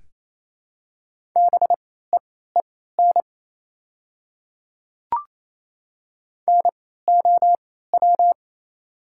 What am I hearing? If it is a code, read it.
Morse code, heard as a keyed tone, about 28 wpm, spaced as for 16 wpm, 700 Hz. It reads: BEEN NOW
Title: 3x Character Spacing - Top 100 Words - 28wpm